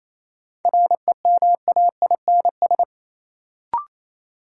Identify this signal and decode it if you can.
Morse code, heard as a keyed tone, about 28 wpm, 700 Hz. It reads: REMAINS
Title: Sets of 5 Words Encore - New Word Drill - 28wpm